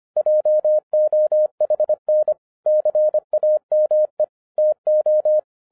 Morse code, heard as a keyed tone, about 24 wpm, 610 Hz. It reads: JOHN CAME TO